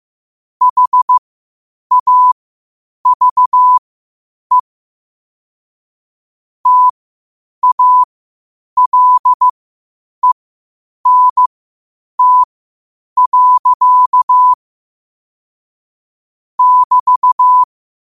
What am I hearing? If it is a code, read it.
Morse code, heard as a keyed tone, about 15 wpm, spaced as for 8 wpm, 1000 Hz. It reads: HAVE TALENT. <BT>
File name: qotdfiles / 8wpm